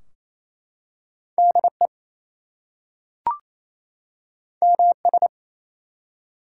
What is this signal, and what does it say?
Morse code, heard as a keyed tone, about 28 wpm, 700 Hz. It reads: DE MS
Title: US States and Canadian Provinces 28wpm